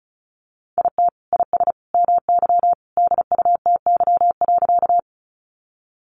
Morse code, heard as a keyed tone, about 34 wpm, 700 Hz. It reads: IT IS MY DUTY.
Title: Sentences from Top 1500 Words 35wpm